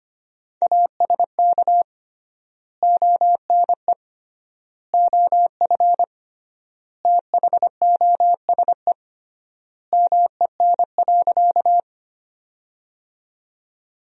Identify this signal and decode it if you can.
Morse code, heard as a keyed tone, about 25 wpm, 700 Hz. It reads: ASK ONE OF THOSE MEN.
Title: 3x Word-Spacing - Sentences from Top 300 Words - 25wpm